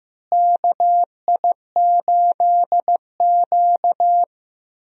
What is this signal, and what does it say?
Morse code, heard as a keyed tone, about 15 wpm, 700 Hz. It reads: KI8Q
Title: USA Call Signs - 4 Characters - Repeated 2x - 15wpm